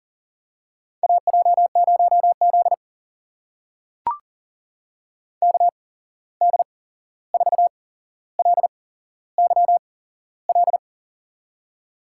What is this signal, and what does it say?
Morse code, heard as a keyed tone, about 39 wpm, spaced as for 15 wpm, 700 Hz. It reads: AJ0Z KD4LYL
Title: USA Call Signs - 8x Character Spacing - 40wpm